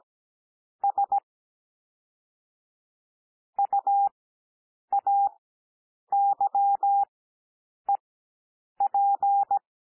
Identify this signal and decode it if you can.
Morse code, heard as a keyed tone, about 17 wpm, spaced as for 8 wpm, 800 Hz. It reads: S UAYEP
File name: cwmp3 / L2 SOUND LETTERS 17 wpm